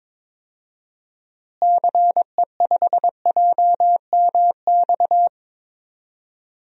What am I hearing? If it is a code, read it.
Morse code, heard as a keyed tone, about 22 wpm, 700 Hz. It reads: CE5JMX
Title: DXCC-100 Call Signs - 6 Characters - Repeated 2x - 22wpm